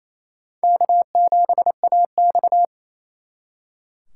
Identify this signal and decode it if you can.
Morse code, heard as a keyed tone, about 35 wpm, 700 Hz. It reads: K7AX